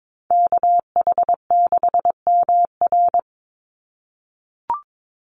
Morse code, heard as a keyed tone, about 22 wpm, 700 Hz. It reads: KH6MR
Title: USA Call Signs 22wpm